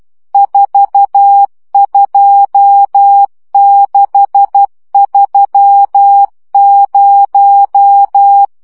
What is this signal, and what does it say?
Morse code, heard as a keyed tone, about 12 wpm, 800 Hz. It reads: 42630
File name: morze vizsga 2011-3 10